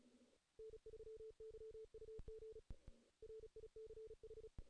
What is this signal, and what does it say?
Morse code, heard as a keyed tone, about 37 wpm, 440 Hz. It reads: N3YUG RICH
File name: 2016 JAN CW NAQP / 2016 Jan naqp k5ot - 0-2